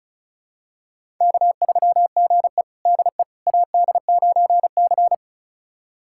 Morse code, heard as a keyed tone, about 36 wpm, 700 Hz. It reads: K3GE DE AD9C
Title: USA Callsign DE Callsign - 4 Characters - 35wpm